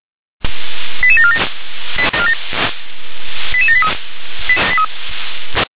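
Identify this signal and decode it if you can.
Five-tone selective call (ZVEI), heard as quick runs of five tones, tones of about 70 ms. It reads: 88747 88783
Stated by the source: ZVEI five-tone call received on SDR.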